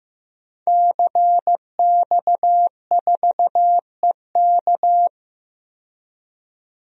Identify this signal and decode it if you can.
Morse code, heard as a keyed tone, about 15 wpm, 700 Hz. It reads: CX4EK
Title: DXCC-100 Call Signs - 5 Characters - 15wpm